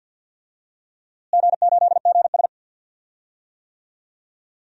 Morse code, heard as a keyed tone, about 46 wpm, 700 Hz. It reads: G8GS
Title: DXCC-100 Call Signs - 4 Characters - Repeated 3x - 50wpm